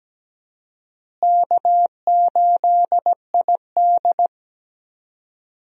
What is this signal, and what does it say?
Morse code, heard as a keyed tone, about 17 wpm, 700 Hz. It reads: K8ID
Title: USA Call Signs - 4 Characters - Repeated 2x - 17wpm